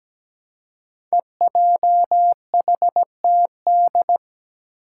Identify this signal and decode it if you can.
Morse code, heard as a keyed tone, about 17 wpm, 700 Hz. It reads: EJHTD
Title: Five Challenging Letter-Number Combinations 17wpm